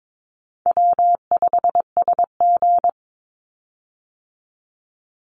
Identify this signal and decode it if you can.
Morse code, heard as a keyed tone, about 22 wpm, 700 Hz. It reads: W5SG
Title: Call Signs - 22wpm